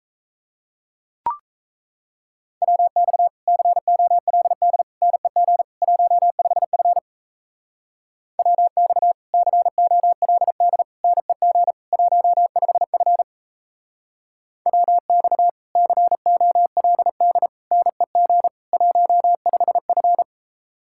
Morse code, heard as a keyed tone, about 36 wpm, 700 Hz. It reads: WX COLD NEG 15F WX COLD NEG 15F WX COLD NEG 15F